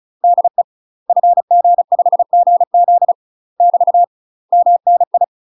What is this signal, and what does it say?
Morse code, heard as a keyed tone, about 36 wpm, 700 Hz. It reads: DE FG5GZ <BT> MNI
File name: UK MASTER QSOs TRAINING / 35WPM / UK MASTER QSOs 0020